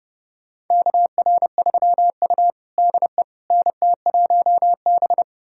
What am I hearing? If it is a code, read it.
Morse code, heard as a keyed tone, about 30 wpm, 700 Hz. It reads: KR3U DE NT1B
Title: USA Callsign DE Callsign - 4 Characters - 30wpm